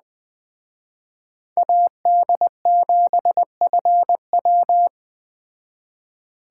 Morse code, heard as a keyed tone, about 20 wpm, 700 Hz. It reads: AD7FW
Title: Speed Racing - USA Call Signs 17wpm